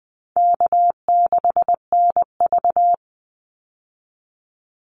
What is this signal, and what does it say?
Morse code, heard as a keyed tone, about 20 wpm, 700 Hz. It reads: K6NV
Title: Easy Call Signs Repeated 2x - 20wpm